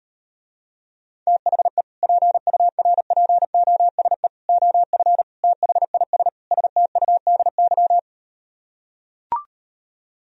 Morse code, heard as a keyed tone, about 39 wpm, 700 Hz. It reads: THE PURPOSE OF THIS STUDY